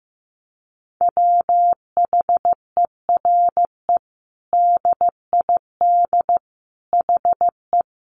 Morse code, heard as a keyed tone, about 15 wpm, 700 Hz. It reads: WHERE DID HE
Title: Sentences from Top 200 Words 15wpm